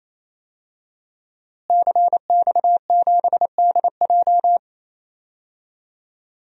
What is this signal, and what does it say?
Morse code, heard as a keyed tone, about 28 wpm, 700 Hz. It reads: CX7DJ
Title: DXCC-100 Call Signs - 5 Characters - Repeated 3x - 28wpm